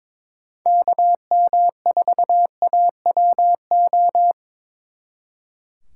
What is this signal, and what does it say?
Morse code, heard as a keyed tone, about 25 wpm, 700 Hz. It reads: KM4AWO
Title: Speed Racing - USA 2x3 Call Signs 22wpm